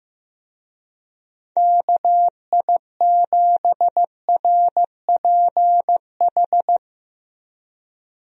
Morse code, heard as a keyed tone, about 15 wpm, 700 Hz. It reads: KI7RPH
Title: USA Call Signs - 6 Characters - 15wpm